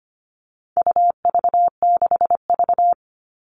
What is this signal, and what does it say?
Morse code, heard as a keyed tone, about 25 wpm, 700 Hz. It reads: UV6V
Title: All Cut Numbers - 4-Digit 25wpm